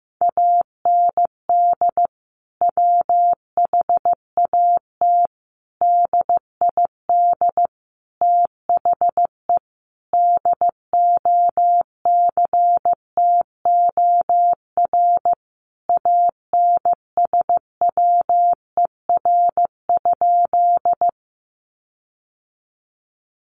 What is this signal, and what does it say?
Morse code, heard as a keyed tone, about 15 wpm, 700 Hz. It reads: AND WHAT DID THE DOCTOR ANSWER?
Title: Sentences from Top 1500 Words 15wpm